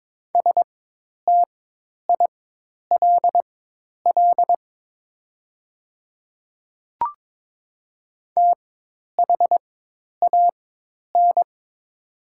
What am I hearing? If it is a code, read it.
Morse code, heard as a keyed tone, about 22 wpm, spaced as for 10 wpm, 690 Hz. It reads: STILL THAN
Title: 4x Character Spacing - Top 300 Words - 22wpm